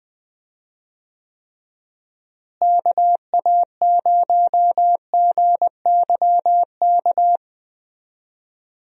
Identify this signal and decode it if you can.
Morse code, heard as a keyed tone, about 20 wpm, 700 Hz. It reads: KA0GYK
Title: USA Call Signs - 6 Characters - Repeated 2x - 20wpm